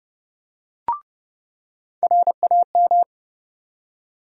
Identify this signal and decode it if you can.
Morse code, heard as a keyed tone, about 30 wpm, 700 Hz. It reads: RAM
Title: USA Call Sign Postfixes - 3 Letters 30wpm